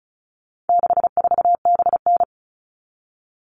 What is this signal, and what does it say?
Morse code, heard as a keyed tone, about 34 wpm, 700 Hz. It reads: 64BN
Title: All Cut Numbers - 4-Digit 35wpm